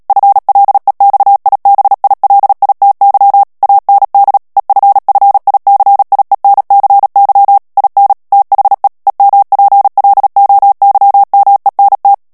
Morse code, heard as a keyed tone, about 37 wpm, 800 Hz. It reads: FLEXIBILITY AND EFFICIENCY IN THE EMPLOYMENT
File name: morse mp3 / ANGLAIS texte clair 3 vitesse 35 mots minute